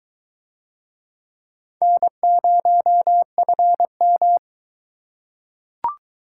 Morse code, heard as a keyed tone, about 22 wpm, 700 Hz. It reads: N0FM